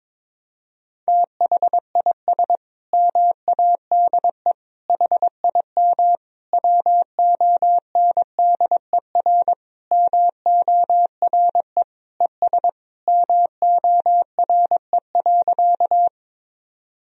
Morse code, heard as a keyed tone, about 22 wpm, 700 Hz. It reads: THIS MADE HIM WONDER MORE ES MORE.
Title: Sentences from Top 500 Words and Abbreviations 22wpm